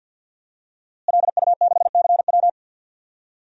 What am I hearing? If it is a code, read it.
Morse code, heard as a keyed tone, about 51 wpm, 700 Hz. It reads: LU6CW
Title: DXCC-100 Call Signs - 5 Characters - Repeated 2x - 50wpm